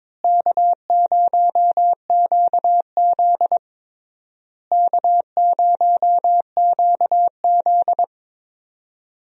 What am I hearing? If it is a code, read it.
Morse code, heard as a keyed tone, about 22 wpm, 700 Hz. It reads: K0QZ K0QZ
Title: USA Call Signs - 4 Characters - Repeated 3x - 22wpm